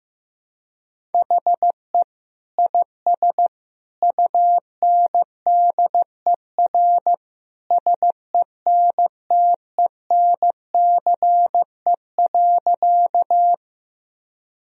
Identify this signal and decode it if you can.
Morse code, heard as a keyed tone, about 16 wpm, 700 Hz. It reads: HE IS UNDER SENTENCE.
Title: Sentences from Top 1000 Words - Review Part 1 - 15wpm